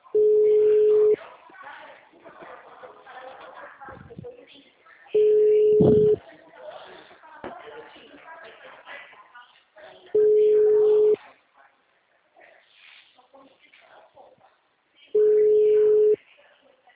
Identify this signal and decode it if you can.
Ringback tone (Europe, 425 Hz), with speech.
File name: out- -2010- - - .53880